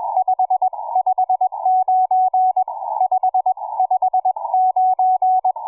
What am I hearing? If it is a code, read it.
Morse code, heard as a keyed tone, about 21 wpm, spaced as for 18 wpm, 740 Hz, with radio noise. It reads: 559 559
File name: cw forum / pj4 k8gg 1827 160m 05h26